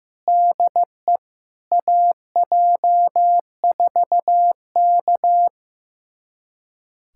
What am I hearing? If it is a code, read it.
Morse code, heard as a keyed tone, about 15 wpm, 700 Hz. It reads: DE AJ4K
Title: USA Callsign DE Callsign - 4 Characters - 15wpm